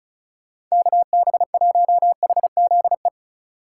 Morse code, heard as a keyed tone, about 36 wpm, 700 Hz. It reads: KB1HZE
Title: USA Call Signs - 6 Characters - 2x Character Spacing - 35wpm